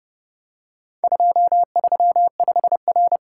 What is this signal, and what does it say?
Morse code, heard as a keyed tone, about 30 wpm, 700 Hz. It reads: 235R